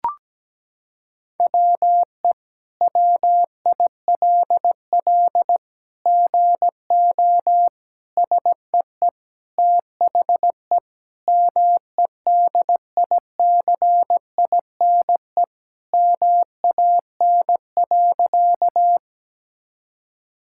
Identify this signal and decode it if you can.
Morse code, heard as a keyed tone, about 17 wpm, 700 Hz. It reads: WE WILL GO SEE THE MEDICINE MAN.
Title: Sentences from Top 1000 Words - Review Part 6 - 17wpm